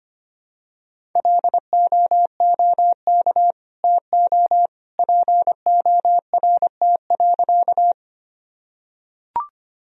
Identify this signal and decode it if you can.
Morse code, heard as a keyed tone, about 25 wpm, 700 Hz. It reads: LOOK TO PORT.
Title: Sentences from Top 1000 Words - Review Part 1 - 25wpm